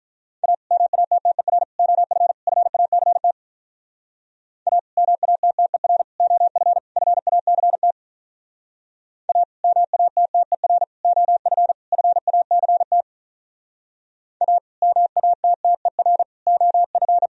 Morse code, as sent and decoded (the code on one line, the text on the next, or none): .- / -- .- - - . .-. / --- ..-. / ..-. .- -.-. - / .- / -- .- - - . .-. / --- ..-. / ..-. .- -.-. - / .- / -- .- - - . .-. / --- ..-. / ..-. .- -.-. - / .- / -- .- - - . .-. / --- ..-.
A MATTER OF FACT A MATTER OF FACT A MATTER OF FACT A MATTER OF